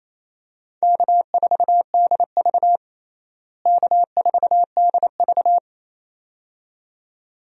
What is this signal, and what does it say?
Morse code, heard as a keyed tone, about 28 wpm, 700 Hz. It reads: K4DV K4DV
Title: USA Call Signs - 4 Characters - Repeated 2x - 28wpm